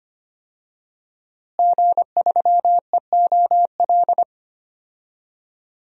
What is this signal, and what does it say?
Morse code, heard as a keyed tone, about 25 wpm, 700 Hz. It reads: G3EOL
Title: DXCC-100 Call Signs - 5 Characters - Repeated 3x - 25wpm